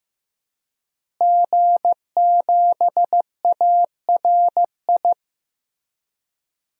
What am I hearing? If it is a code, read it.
Morse code, heard as a keyed tone, about 15 wpm, 700 Hz. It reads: G7ARI
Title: DXCC-100 Call Signs - 5 Characters - Repeated 3x - 15wpm